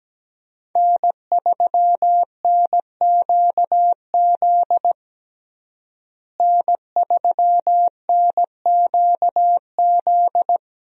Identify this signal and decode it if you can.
Morse code, heard as a keyed tone, about 17 wpm, 700 Hz. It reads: N3NQZ N3NQZ
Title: USA Call Signs - 5 Characters - Repeated 3x - 17wpm